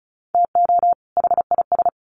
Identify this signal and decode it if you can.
Morse code, heard as a keyed tone, about 34 wpm, 700 Hz. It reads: TO HIS
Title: Sets of 2 Words 35wpm